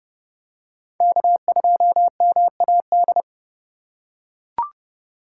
Morse code, heard as a keyed tone, about 30 wpm, 700 Hz. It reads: K2MAD
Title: USA Call Signs - 2x Character Spacing - 30wpm